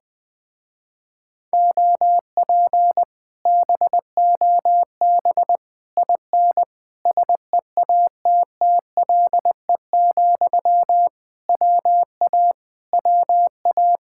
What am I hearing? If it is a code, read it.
Morse code, heard as a keyed tone, about 20 wpm, 700 Hz. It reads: OP BOB IN SEATTLE, WA WA
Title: Speed-Racing Common QSO Phrases - 20wpm